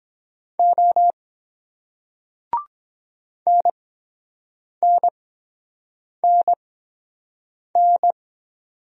Morse code, heard as a keyed tone, about 24 wpm, spaced as for 7 wpm, 700 Hz. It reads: O NNNN